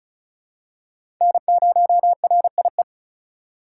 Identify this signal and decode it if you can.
Morse code, heard as a keyed tone, about 36 wpm, 700 Hz. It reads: N0RIE
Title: USA Call Signs - 5 Characters - 4x Character Spacing - 35wpm